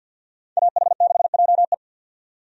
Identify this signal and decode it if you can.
Morse code, heard as a keyed tone, about 52 wpm, 700 Hz. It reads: AH6JE
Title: USA Call Signs - 5 Characters - 50wpm